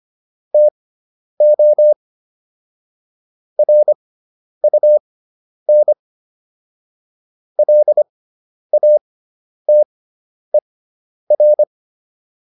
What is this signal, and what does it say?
Morse code, heard as a keyed tone, about 25 wpm, spaced as for 10 wpm, 600 Hz. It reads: TO RUN LATER